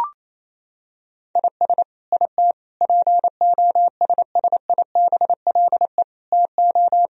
Morse code, heard as a keyed tone, about 28 wpm, 700 Hz. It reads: IS IT POSSIBLE TO
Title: Sentences from Top 1000 Words - Review Part 3 - 28wpm